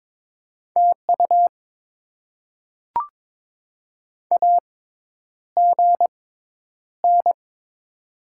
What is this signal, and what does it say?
Morse code, heard as a keyed tone, about 22 wpm, spaced as for 10 wpm, 700 Hz. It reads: TU AGN
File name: Top 20 QSO Elements - 6x Character Spacing - 22wpm